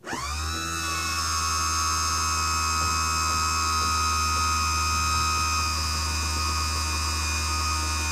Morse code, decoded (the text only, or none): none